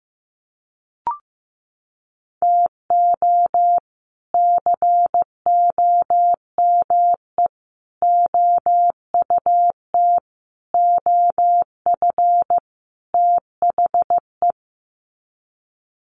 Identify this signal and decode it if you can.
Morse code, heard as a keyed tone, about 15 wpm, 700 Hz. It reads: TO COME OUT OF THE
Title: Sets of 5 Words Encore2 15wpm